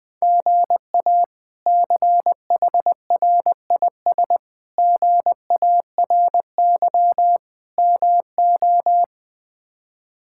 Morse code, heard as a keyed tone, about 21 wpm, 700 Hz. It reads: GA CHRIS GARY MO
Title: K1USN SST QSO Practice - 20wpm